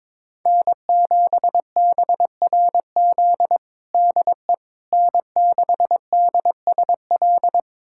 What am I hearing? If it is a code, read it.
Morse code, heard as a keyed tone, about 22 wpm, 700 Hz. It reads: N7BRZ DE N6DSL
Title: USA Callsign DE Callsign - 5 Characters - 22wpm